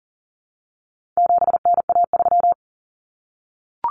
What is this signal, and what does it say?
Morse code, heard as a keyed tone, about 40 wpm, 700 Hz. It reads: 7NA3
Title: Common Cut Numbers - 4-Digit 40wpm